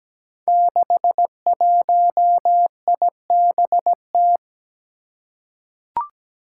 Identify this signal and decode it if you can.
Morse code, heard as a keyed tone, about 17 wpm, 700 Hz. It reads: 61IBT
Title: Five Challenging Letter-Number Combinations 17wpm